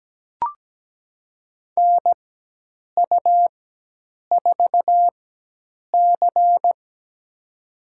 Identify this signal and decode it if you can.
Morse code, heard as a keyed tone, about 17 wpm, spaced as for 8 wpm, 700 Hz. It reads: NU4C
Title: USA Call Signs - 4x Character Spacing - 17wpm